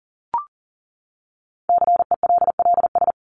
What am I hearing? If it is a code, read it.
Morse code, heard as a keyed tone, about 40 wpm, 700 Hz. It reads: CELLS
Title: ES:S Suffix - Top 100 Words with Base Words 40wpm